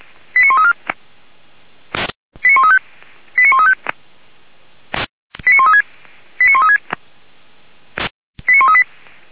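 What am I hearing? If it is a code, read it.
Five-tone selective call (ZVEI), heard as quick runs of five tones, tones of about 70 ms. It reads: 89134 89135 89135 89136 89136 89138
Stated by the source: ZVEI five-tone selective call 89133 received on an SDR, from the Feuerwehr Orlamünde audio server.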